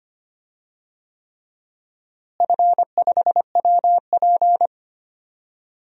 Morse code, heard as a keyed tone, about 25 wpm, 700 Hz. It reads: F5WP
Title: DXCC-100 Call Signs - 4 Characters - Repeated 3x - 25wpm